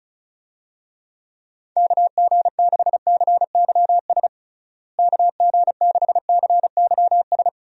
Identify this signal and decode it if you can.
Morse code, heard as a keyed tone, about 34 wpm, 700 Hz. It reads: KG6CYS KG6CYS
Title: USA Call Signs - 6 Characters - Repeated 2x - 35wpm